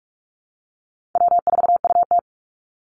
Morse code, heard as a keyed tone, about 44 wpm, 700 Hz. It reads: W4UT
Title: Easy Call Signs Repeated 3x - 45wpm